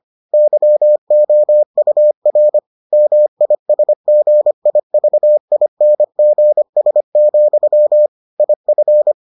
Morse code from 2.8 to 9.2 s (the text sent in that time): MISGIVINGS, IF